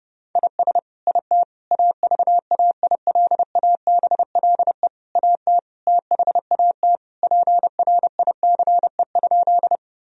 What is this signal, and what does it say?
Morse code, heard as a keyed tone, about 32 wpm, 700 Hz. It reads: IS IT AVAILABLE AT THAT PRICE?
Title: Sentences from Top 1000 Words - Review Part 3 - 30wpm